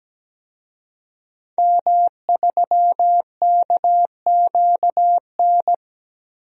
Morse code, heard as a keyed tone, about 17 wpm, 700 Hz. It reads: M3KQN
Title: DXCC-100 Call Signs - 5 Characters - Repeated 3x - 17wpm